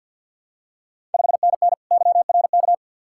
Morse code, heard as a keyed tone, about 50 wpm, 700 Hz. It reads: 5NN YRK
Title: Ontario QSO Party Exchange 50wpm